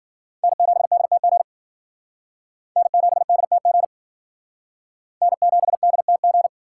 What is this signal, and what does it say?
Morse code, heard as a keyed tone, about 52 wpm, 700 Hz. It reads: N7DTG N7DTG N7DTG